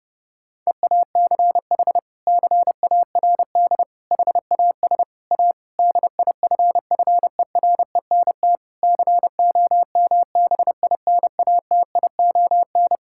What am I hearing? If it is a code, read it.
Morse code, heard as a keyed tone, about 32 wpm, 700 Hz. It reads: EACH CARD HAS A DIFFERENT COMBINATION
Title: Sentences from Top 1000 Words - Review Part 6 - 30wpm